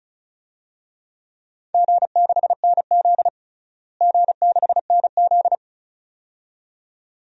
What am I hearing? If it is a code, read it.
Morse code, heard as a keyed tone, about 34 wpm, 700 Hz. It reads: G6NZ G6NZ
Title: DXCC-100 Call Signs - 4 Characters - Repeated 2x - 35wpm